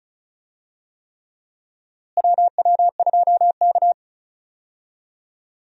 Morse code, heard as a keyed tone, about 36 wpm, 700 Hz. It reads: WW2K
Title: USA Call Signs - 4 Characters - Repeated 3x - 35wpm